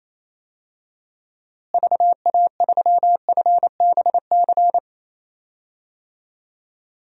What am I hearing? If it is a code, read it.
Morse code, heard as a keyed tone, about 27 wpm, 700 Hz. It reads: VA3FBC
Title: DXCC-100 Call Signs - 6 Characters - Repeated 2x - 28wpm